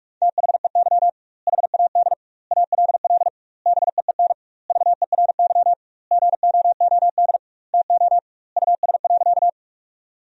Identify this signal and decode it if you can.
Morse code, heard as a keyed tone, about 46 wpm, 700 Hz. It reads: THEY HAD ALL BEEN VERY GOOD TO US.
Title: 2x Word-Spacing - Sentences from Top 200 Words - 45wpm